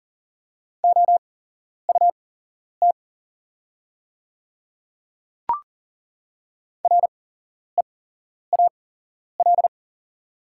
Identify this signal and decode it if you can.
Morse code, heard as a keyed tone, about 40 wpm, spaced as for 11 wpm, 690 Hz. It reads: OUT REAL
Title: 8x Character Spacing - Top 300 Words - 40wpm